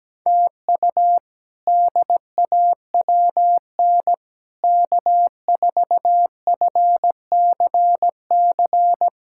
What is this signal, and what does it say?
Morse code, heard as a keyed tone, about 17 wpm, 700 Hz. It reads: TU DAWN K4FCC
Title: K1USN SST QSO Practice - 17wpm